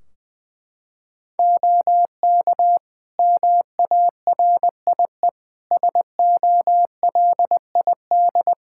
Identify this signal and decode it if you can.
Morse code, heard as a keyed tone, about 20 wpm, 700 Hz. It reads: OK MARIE SOLID
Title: Practice QSO - Exchange 3 - More Info - 20wpm